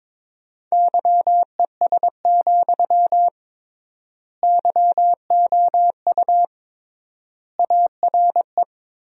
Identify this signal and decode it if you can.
Morse code, heard as a keyed tone, about 21 wpm, 700 Hz. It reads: YES, YOU ARE
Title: 3x Word-Spacing - Sentences from Top 500 Words - 22wpm